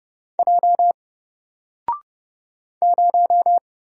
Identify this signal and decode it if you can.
Morse code, heard as a keyed tone, about 31 wpm, 700 Hz. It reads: J 0
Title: Single Letter-Number Alt 30wpm